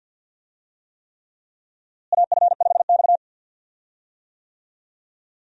Morse code, heard as a keyed tone, about 52 wpm, 700 Hz. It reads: AF5X
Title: USA Call Signs - 4 Characters - Repeated 3x - 50wpm